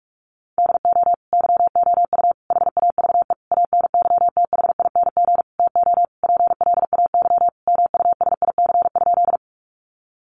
Morse code, heard as a keyed tone, about 46 wpm, 700 Hz. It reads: DO YOU HAVE ANYTHING TO PLAY MUSIC?
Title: Sentences from Top 1000 Words 45wpm